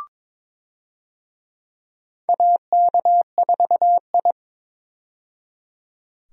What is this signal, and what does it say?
Morse code, heard as a keyed tone, about 22 wpm, 700 Hz. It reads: AK4I
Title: USA Call Signs - 4 Characters - 22wpm